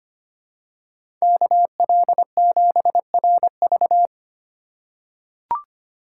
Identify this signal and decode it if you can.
Morse code, heard as a keyed tone, about 25 wpm, 700 Hz. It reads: KL7RV
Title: USA Call Signs - 4x Character Spacing - 25wpm